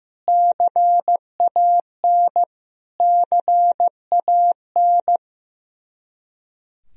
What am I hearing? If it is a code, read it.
Morse code, heard as a keyed tone, about 15 wpm, 700 Hz. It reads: CAN CAN
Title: Top 100 Words Repeated 15wpm